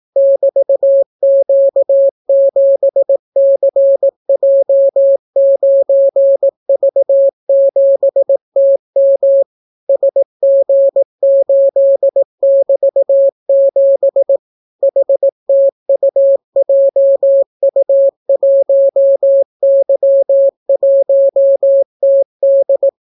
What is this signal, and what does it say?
Morse code, heard as a keyed tone, about 18 wpm, 550 Hz. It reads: <BT>Q7CJ9V7TM SG8<BT>7 HTUJU1Y1TD